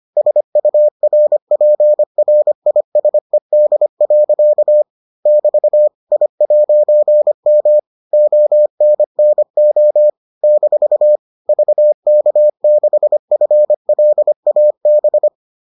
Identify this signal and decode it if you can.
Morse code, heard as a keyed tone, about 25 wpm, 600 Hz. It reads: SURPRISED. <BT> I'M ONNO - VK6FLAB